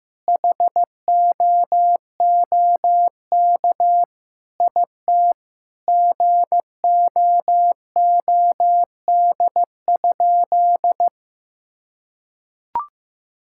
Morse code, heard as a keyed tone, about 15 wpm, 700 Hz. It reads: HOOK IT GOOD?
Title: Sentences from Top 1600 Words 15wpm